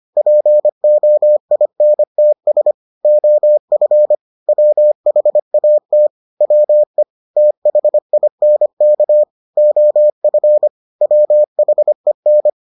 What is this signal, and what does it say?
Morse code, heard as a keyed tone, about 25 wpm, 600 Hz. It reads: POINTS OF WHAT WE THINK OF WHEN